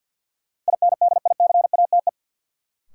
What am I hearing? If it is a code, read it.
Morse code, heard as a keyed tone, about 53 wpm, 700 Hz. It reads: INDICATE
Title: Sentences from Top 1300 Words - New Word Drill - 50wpm